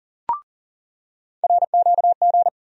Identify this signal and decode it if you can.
Morse code, heard as a keyed tone, about 40 wpm, 700 Hz. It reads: RQG
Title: USA Call Sign Postfixes - 3 Letters 40wpm